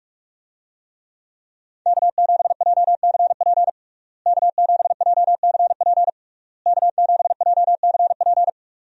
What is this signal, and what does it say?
Morse code, heard as a keyed tone, about 44 wpm, 700 Hz. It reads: K7JCP K7JCP K7JCP
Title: USA Call Signs - 5 Characters - Repeated 3x - 45wpm